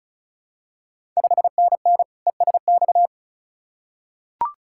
Morse code, heard as a keyed tone, about 34 wpm, 700 Hz. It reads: 5NN ESX
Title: Ontario QSO Party Exchange 35wpm